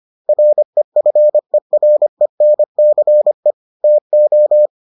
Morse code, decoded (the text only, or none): REFERENCE TO